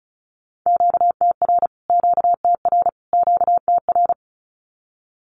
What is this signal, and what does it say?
Morse code, heard as a keyed tone, about 34 wpm, 700 Hz. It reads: QTR QTR QTR
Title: Q-Codes Repeated 3x 35wpm